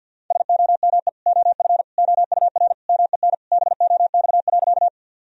Morse code, heard as a keyed tone, about 51 wpm, 700 Hz. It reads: SOME OF OUR MEN BOX.
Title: Sentences from Top 1000 Words - Review Part 2 - 50wpm